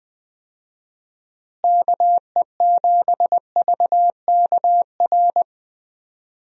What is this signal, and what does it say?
Morse code, heard as a keyed tone, about 20 wpm, 700 Hz. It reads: KE7VKR
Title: USA Call Signs - 6 Characters - Repeated 3x - 20wpm